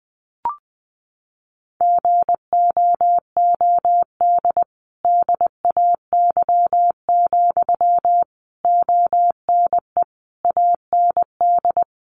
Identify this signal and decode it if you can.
Morse code, heard as a keyed tone, about 20 wpm, 700 Hz. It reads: GOOD DAY, ONE AND
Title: Sentences from Top 200 Words 20wpm